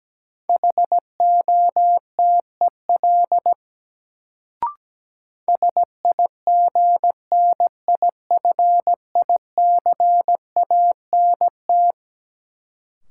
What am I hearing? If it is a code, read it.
Morse code, heard as a keyed tone, about 17 wpm, 700 Hz. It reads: HOTEL SIGNIFICANT
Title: Sentences from Top 1600 Words - New Word Drill Alt - 17wpm